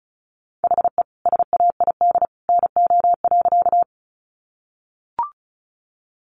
Morse code, heard as a keyed tone, about 34 wpm, 700 Hz. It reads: HE SAID NO.
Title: Sentences from Top 100 Words 35wpm